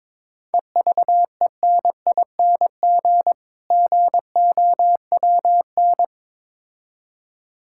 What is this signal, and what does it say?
Morse code, heard as a keyed tone, about 22 wpm, 690 Hz. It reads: EVENING GOWN
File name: Difficult Words with Hints - 22wpm